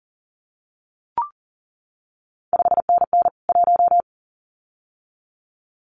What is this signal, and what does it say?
Morse code, heard as a keyed tone, about 38 wpm, 700 Hz. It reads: 5NN 1
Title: CQ WW DX Contest Exchange Encore - 40wpm